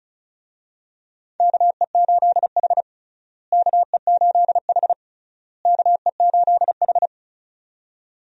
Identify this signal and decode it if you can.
Morse code, heard as a keyed tone, about 36 wpm, 700 Hz. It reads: KE8H KE8H KE8H
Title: USA Call Signs - 4 Characters - Repeated 3x - 35wpm